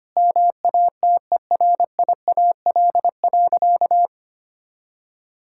Morse code, heard as a keyed tone, about 25 wpm, 700 Hz. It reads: MATERIAL.
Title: Sentences from Top 1000 Words - Review Part 5 - 25wpm